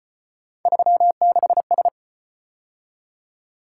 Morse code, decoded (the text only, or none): 36S